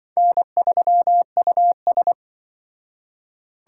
Morse code, heard as a keyed tone, about 24 wpm, 700 Hz. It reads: N3US
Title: Speed Racing - USA 1x2 Call Signs 20wpm